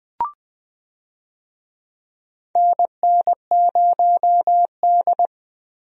Morse code, heard as a keyed tone, about 20 wpm, 700 Hz. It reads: NN0D
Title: USA Call Signs - 4 Characters - Repeated 2x - 20wpm